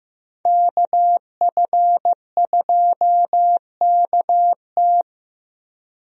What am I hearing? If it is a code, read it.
Morse code, heard as a keyed tone, about 15 wpm, 700 Hz. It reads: KF2KT
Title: USA Call Signs - 5 Characters - Repeated 2x - 15wpm